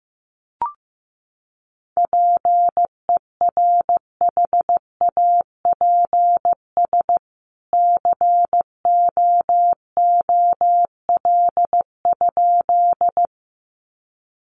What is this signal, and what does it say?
Morse code, heard as a keyed tone, about 15 wpm, 700 Hz. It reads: PERHAPS COOL?
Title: Sentences from Top 1200 Words 15wpm